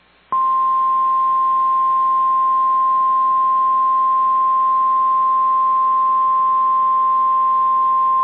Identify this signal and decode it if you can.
Emergency Alert System (SAME) tones (the 1050 Hz attention tone). It is heard from 0.4 to 7.7 s.